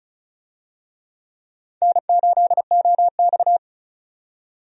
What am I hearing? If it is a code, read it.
Morse code, heard as a keyed tone, about 36 wpm, 700 Hz. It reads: N8OX
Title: USA Call Signs - 4 Characters - 35wpm